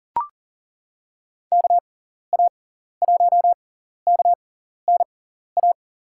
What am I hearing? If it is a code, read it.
Morse code, heard as a keyed tone, about 40 wpm, spaced as for 14 wpm, 700 Hz. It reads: KA1KNA
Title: USA Call Signs - 6 Characters - 6x Character Spacing - 40wpm